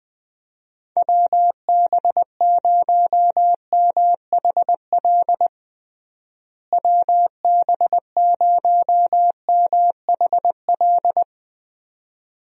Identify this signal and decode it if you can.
Morse code, heard as a keyed tone, about 20 wpm, 700 Hz. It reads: WB0MHL WB0MHL
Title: USA Call Signs - 6 Characters - Repeated 2x - 20wpm